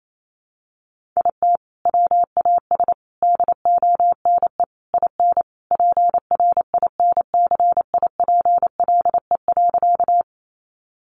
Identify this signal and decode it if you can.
Morse code, heard as a keyed tone, about 28 wpm, 700 Hz. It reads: IT WAS DONE IN PRINCIPLE.
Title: Sentences from Top 1400 Words 28wpm